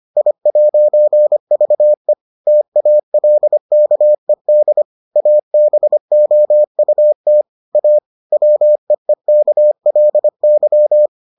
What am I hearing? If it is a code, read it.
Morse code, heard as a keyed tone, about 25 wpm, 600 Hz. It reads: I'VE TALKED ABOUT A WEEKLY